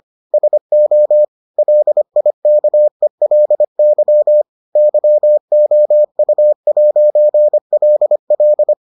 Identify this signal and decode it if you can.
Morse code, heard as a keyed tone, about 25 wpm, 600 Hz. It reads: SO LIKELY YOU'LL